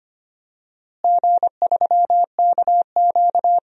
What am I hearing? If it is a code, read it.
Morse code, heard as a keyed tone, about 25 wpm, 700 Hz. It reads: G3KQ
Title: DXCC-100 Call Signs - 4 Characters - Repeated 2x - 25wpm